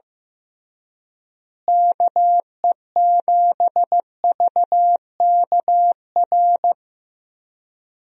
Morse code, heard as a keyed tone, about 16 wpm, 700 Hz. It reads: KE7VKR
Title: USA Call Signs - 6 Characters - Repeated 3x - 15wpm